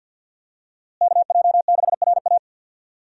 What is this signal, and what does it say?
Morse code, heard as a keyed tone, about 51 wpm, 700 Hz. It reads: KJ6RA
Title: USA Call Signs - 5 Characters - Repeated 3x - 50wpm